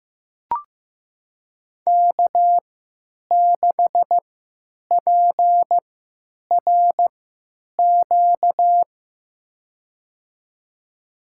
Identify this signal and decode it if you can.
Morse code, heard as a keyed tone, about 15 wpm, spaced as for 8 wpm, 700 Hz. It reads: K6PRQ
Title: USA Call Signs - 5 Characters - 3x Character Spacing - 15wpm